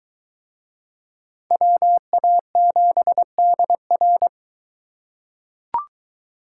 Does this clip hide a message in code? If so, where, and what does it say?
Yes: Morse code from 1.5 to 4.3 s, heard as a keyed tone, about 20 wpm, 700 Hz. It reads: WA7DR